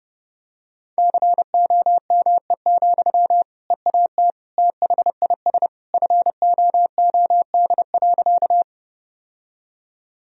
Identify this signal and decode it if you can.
Morse code, heard as a keyed tone, about 30 wpm, 700 Hz. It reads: COME, EAT THIS FOOD.
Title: Sentences from Top 1000 Words - Review Part 1 - 30wpm